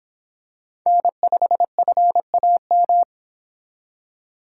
Morse code, heard as a keyed tone, about 25 wpm, 700 Hz. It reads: N5FAM